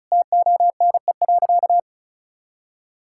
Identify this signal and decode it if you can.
Morse code, heard as a keyed tone, about 36 wpm, 700 Hz. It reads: TONE.
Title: Sentences from Top 1600 Words 35wpm